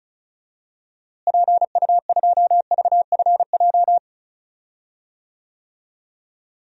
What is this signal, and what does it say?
Morse code, heard as a keyed tone, about 34 wpm, 700 Hz. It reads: PU2VFJ
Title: DXCC-100 Call Signs - 6 Characters - Repeated 2x - 35wpm